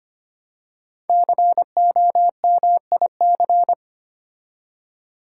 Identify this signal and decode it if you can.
Morse code, heard as a keyed tone, about 25 wpm, 700 Hz. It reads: COMIC
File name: 5 Letter Words - First 3 Letters Given - 25wpm